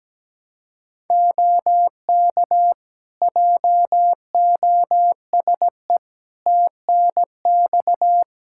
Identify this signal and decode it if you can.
Morse code, heard as a keyed tone, about 17 wpm, 700 Hz. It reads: OK JOSE TNX
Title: K1USN SST QSO Practice 17wpm